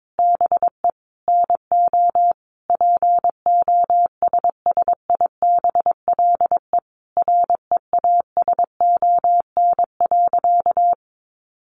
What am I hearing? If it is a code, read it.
Morse code, heard as a keyed tone, about 22 wpm, 700 Hz. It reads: BE NO POSSIBLE REASON.
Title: Sentences from Top 700 Words 22wpm